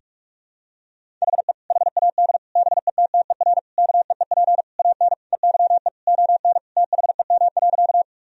Morse code, heard as a keyed tone, about 41 wpm, 700 Hz. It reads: HE HAD BETTER KEEP AN EYE ON THEM.
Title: Sentences from Top 1000 Words - Review Part 1 - 45wpm